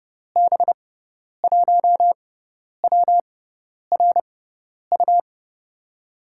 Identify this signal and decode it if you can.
Morse code, heard as a keyed tone, about 30 wpm, spaced as for 10 wpm, 700 Hz. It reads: B1WRU
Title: USA Call Signs - 6 Characters - 6x Character Spacing - 30wpm